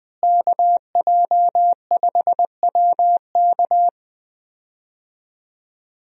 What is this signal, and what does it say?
Morse code, heard as a keyed tone, about 20 wpm, 700 Hz. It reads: KJ5WK
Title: USA Call Signs - 5 Characters - Repeated 2x - 20wpm